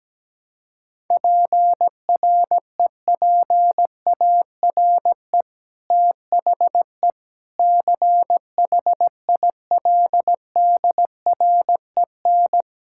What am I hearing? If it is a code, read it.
Morse code, heard as a keyed tone, about 17 wpm, 700 Hz. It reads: PREPARE THE CHILDREN
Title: Sentences from Top 1000 Words - Review Part 6 - 17wpm